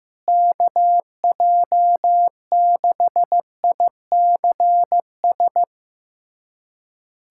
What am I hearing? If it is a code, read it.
Morse code, heard as a keyed tone, about 15 wpm, 700 Hz. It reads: KJ6ICS
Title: USA Callsign DE Callsign 15wpm